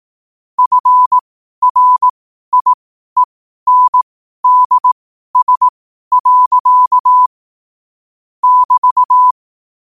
Morse code, heard as a keyed tone, about 18 wpm, spaced as for 12 wpm, 1000 Hz. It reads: FRIENDS. <BT>
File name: qotdfiles / 12wpm